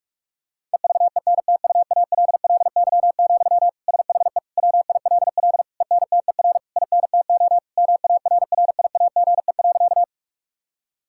Morse code, heard as a keyed tone, about 46 wpm, 700 Hz. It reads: EVENTUALLY, SHE WILL ENTER INTO MARRIAGE.
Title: Sentences from Top 1000 Words - Review Part 6 - 45wpm